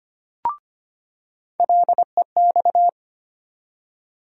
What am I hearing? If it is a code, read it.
Morse code, heard as a keyed tone, about 25 wpm, 700 Hz. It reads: LEX
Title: USA Call Sign Postfixes - 3 Letters 25wpm